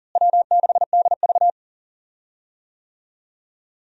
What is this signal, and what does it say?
Morse code, heard as a keyed tone, about 37 wpm, 700 Hz. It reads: W6DV
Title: USA Call Signs - 4 Characters - Repeated 2x - 40wpm